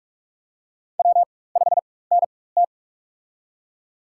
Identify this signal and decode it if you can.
Morse code, heard as a keyed tone, about 46 wpm, spaced as for 21 wpm, 700 Hz. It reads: W5NT
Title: USA Call Signs - 4 Characters - 4x Character Spacing - 45wpm